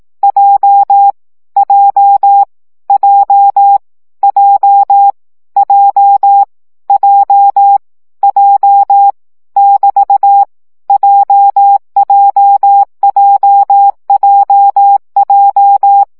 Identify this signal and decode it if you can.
Morse code, heard as a keyed tone, about 18 wpm, 800 Hz. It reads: J J J J J J J <BT> JJJJJ
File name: CW gyakorlas j 15 baud